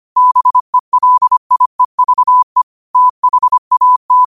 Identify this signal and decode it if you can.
Morse code, heard as a keyed tone, about 25 wpm, 1000 Hz. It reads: BELIEVE THAT
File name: qotdfiles / 25wpm